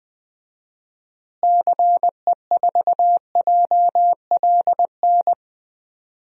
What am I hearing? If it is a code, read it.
Morse code, heard as a keyed tone, about 19 wpm, 700 Hz. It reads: CE4JLN
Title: DXCC-100 Call Signs - 6 Characters - Repeated 3x - 20wpm